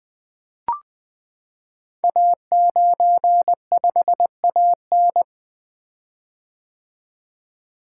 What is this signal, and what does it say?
Morse code, heard as a keyed tone, about 20 wpm, 700 Hz. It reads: A95AN
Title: Code Groups - 5 Letter:Numbers - 20wpm